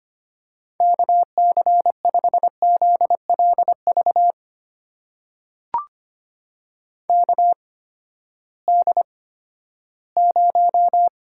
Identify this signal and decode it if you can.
Morse code, heard as a keyed tone, about 25 wpm, spaced as for 14 wpm, 700 Hz. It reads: KC5ZLV KD0
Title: USA Call Signs - 6 Characters - 8x Character Spacing - 25wpm